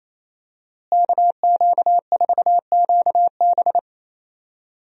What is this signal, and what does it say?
Morse code, heard as a keyed tone, about 28 wpm, 700 Hz. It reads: KQ4QB
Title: USA Call Signs - 5 Characters - Repeated 2x - 28wpm